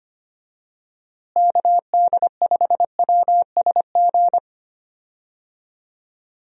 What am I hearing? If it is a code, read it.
Morse code, heard as a keyed tone, about 25 wpm, 700 Hz. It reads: KD5WSG
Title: USA Call Signs - 6 Characters - Repeated 3x - 25wpm